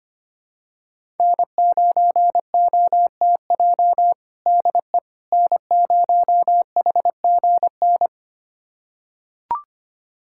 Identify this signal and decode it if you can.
Morse code, heard as a keyed tone, about 25 wpm, 700 Hz. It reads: N9OTJ DE N0HGN
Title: USA Callsign DE Callsign - 5 Characters - 25wpm